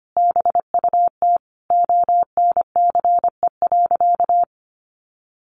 Morse code, heard as a keyed tone, about 25 wpm, 700 Hz. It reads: BUT ONCE.
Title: Sentences from Top 1300 Words 25wpm